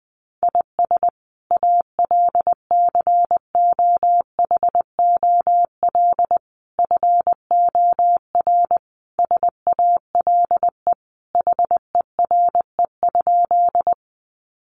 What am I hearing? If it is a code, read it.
Morse code, heard as a keyed tone, about 20 wpm, 700 Hz. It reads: IS ALCOHOL FOR SALE HERE?
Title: Sentences from Top 1400 Words 20wpm